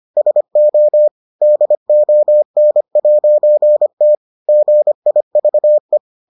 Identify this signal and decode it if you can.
Morse code, heard as a keyed tone, about 25 wpm, 600 Hz. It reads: SO DON'T GIVE